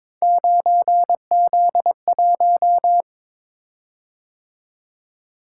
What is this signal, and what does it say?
Morse code, heard as a keyed tone, about 22 wpm, 700 Hz. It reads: 9Z1
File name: practice / 22wpm / Three Letter-Number Combinations - 22wpm